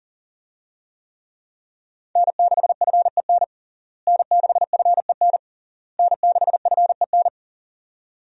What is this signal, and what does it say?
Morse code, heard as a keyed tone, about 37 wpm, 700 Hz. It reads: N6FEN N6FEN N6FEN
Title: USA Call Signs - 5 Characters - Repeated 3x - 40wpm